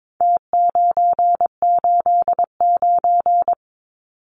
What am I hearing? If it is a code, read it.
Morse code, heard as a keyed tone, about 22 wpm, 700 Hz. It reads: T989
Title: Numbers - Contest Style - 4-Digit 22wpm